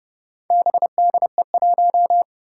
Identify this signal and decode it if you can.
Morse code, heard as a keyed tone, about 31 wpm, 700 Hz. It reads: BDE1
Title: Four Challenging Letter-Number Combinations 30wpm